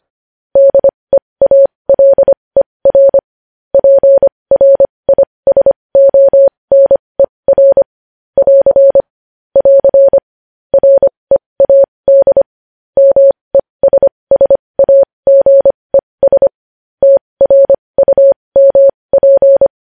Morse code, heard as a keyed tone, about 25 wpm, spaced as for 20 wpm, 550 Hz. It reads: DEALER PRISONER <AR> <AR> READ MESSAGES TRUMP
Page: News Headlines in Morse Code at 20 WPM